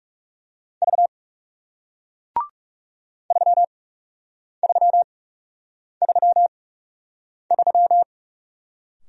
Morse code, heard as a keyed tone, about 40 wpm, spaced as for 8 wpm, 700 Hz. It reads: V 3333